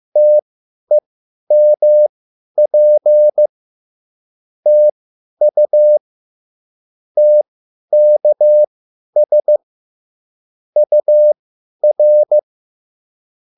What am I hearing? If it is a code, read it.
Morse code, heard as a keyed tone, about 15 wpm, spaced as for 10 wpm, 600 Hz. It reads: TEMP TU TKS UR